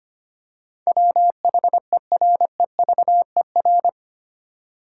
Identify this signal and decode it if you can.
Morse code, heard as a keyed tone, about 25 wpm, 700 Hz. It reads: WHEREVER
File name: 8 Letter Words - First 3 Letters Given - 25wpm